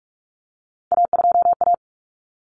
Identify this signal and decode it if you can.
Morse code, heard as a keyed tone, about 44 wpm, 700 Hz. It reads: A2A
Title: Common Cut Numbers - 3-Digit 45wpm